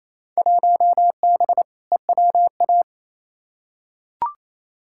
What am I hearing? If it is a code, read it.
Morse code, heard as a keyed tone, about 28 wpm, 700 Hz. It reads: 1B EWA
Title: Field Day Exchanges Encore - 28wpm